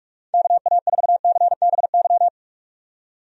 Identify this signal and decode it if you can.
Morse code, heard as a keyed tone, about 45 wpm, 700 Hz. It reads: KA4CBY
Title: USA Call Signs - 6 Characters - 45wpm